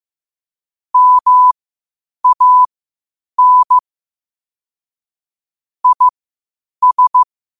Morse code, heard as a keyed tone, about 15 wpm, spaced as for 8 wpm, 1000 Hz. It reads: MAN IS
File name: qotdfiles / 8wpm